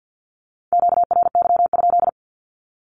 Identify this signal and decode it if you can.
Morse code, heard as a keyed tone, about 48 wpm, 700 Hz. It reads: QRY?
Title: Q-Codes Repeated 3x 50wpm